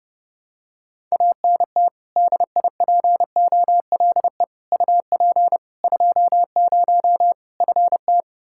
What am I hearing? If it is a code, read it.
Morse code, heard as a keyed tone, about 30 wpm, 700 Hz. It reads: ANT DIPOLE UP 20 FT
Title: Practice QSO - Exchange 2 - Info - 30wpm